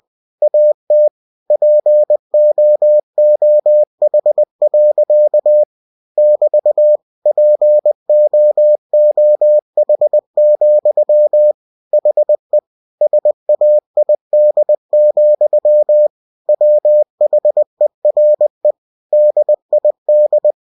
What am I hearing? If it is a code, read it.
Morse code, heard as a keyed tone, about 20 wpm, 600 Hz. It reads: AT POOH. <BT> POOH, HE SAID, WHERE DID
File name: cw-books / winnie-the-pooh / winnie-20wpm-page-0087